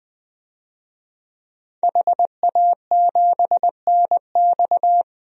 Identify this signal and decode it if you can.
Morse code, heard as a keyed tone, about 20 wpm, 700 Hz. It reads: HA7NX
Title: DXCC-100 Call Signs - 5 Characters - 20wpm